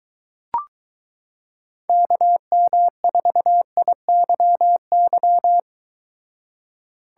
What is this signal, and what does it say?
Morse code, heard as a keyed tone, about 22 wpm, 700 Hz. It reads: KM4IYY